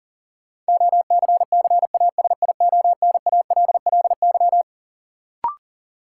Morse code, heard as a keyed tone, about 40 wpm, 700 Hz. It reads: OCCASIONALLY
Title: Sentences from Top 1300 Words - New Word Drill Alt - 40wpm